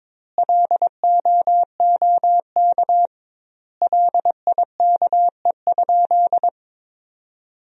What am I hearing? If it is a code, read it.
Morse code, heard as a keyed tone, about 22 wpm, 700 Hz. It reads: LOOK LIKE?
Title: 2x Word-Spacing - Sentences from Top 100 Words - 22wpm